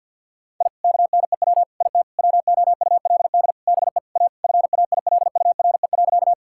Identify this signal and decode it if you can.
Morse code, heard as a keyed tone, about 52 wpm, 700 Hz. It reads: I KNEW IT WOULD BE A FAILURE.
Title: Sentences from Top 1000 Words - Review Part 6 - 50wpm